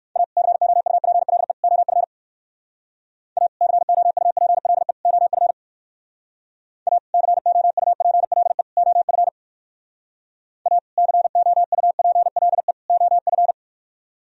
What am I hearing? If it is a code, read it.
Morse code, heard as a keyed tone, about 58 wpm, 700 Hz. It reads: A COUPLE OF A COUPLE OF A COUPLE OF A COUPLE OF